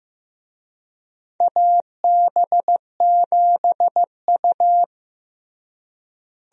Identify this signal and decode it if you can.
Morse code, heard as a keyed tone, about 16 wpm, 700 Hz. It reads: AB7U
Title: USA Call Signs - 4 Characters - Repeated 3x - 15wpm